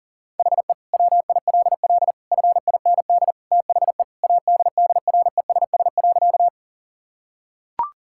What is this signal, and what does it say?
Morse code, heard as a keyed tone, about 40 wpm, 700 Hz. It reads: HE WILL FIND THE ADDRESS.
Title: Sentences from Top 1000 Words - Review Part 6 - 40wpm